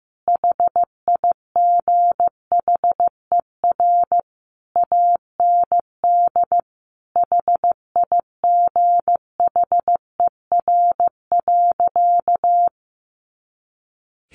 Morse code, heard as a keyed tone, about 15 wpm, 700 Hz. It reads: HIGHER AND HIGHER.
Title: Additional Easy Phrases 15wpm